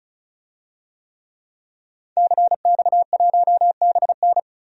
Morse code, heard as a keyed tone, about 34 wpm, 700 Hz. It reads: CX1BN
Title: DXCC-100 Call Signs - 5 Characters - 35wpm